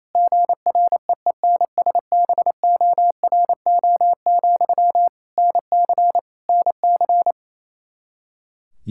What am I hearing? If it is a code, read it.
Morse code, heard as a keyed tone, about 28 wpm, 700 Hz. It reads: GREENSBORO, NC NC
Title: Practice QSO - Exchange 1 - Beginning - 28wpm